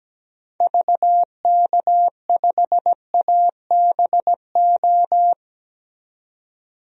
Morse code, heard as a keyed tone, about 17 wpm, 700 Hz. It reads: VK5ABO
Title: DXCC-100 Call Signs - 6 Characters - 17wpm